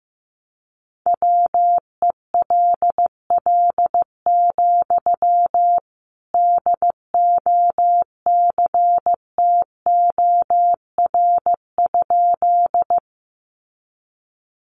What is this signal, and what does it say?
Morse code, heard as a keyed tone, about 15 wpm, 700 Hz. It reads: WELL, DOCTOR?
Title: Sentences from Top 1500 Words 15wpm